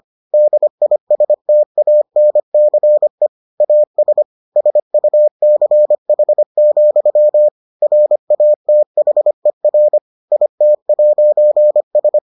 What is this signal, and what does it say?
Morse code, heard as a keyed tone, about 25 wpm, 600 Hz. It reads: DISTANCE AS SUCH, RATHER IT'S